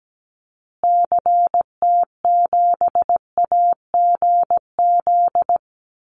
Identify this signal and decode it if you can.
Morse code, heard as a keyed tone, about 17 wpm, 700 Hz. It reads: CT7AGZ
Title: Call Signs Repeated 2x - 17wpm